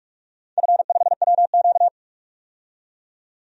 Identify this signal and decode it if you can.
Morse code, heard as a keyed tone, about 44 wpm, 700 Hz. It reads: F5WQ
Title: DXCC-100 Call Signs - 4 Characters - 45wpm